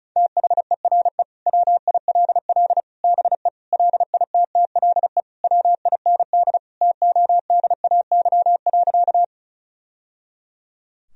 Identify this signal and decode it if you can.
Morse code, heard as a keyed tone, about 34 wpm, 700 Hz. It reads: THERE WILL BE LITTLE WIND TODAY.
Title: Sentences from Top 1000 Words - Review Part 2 - 35wpm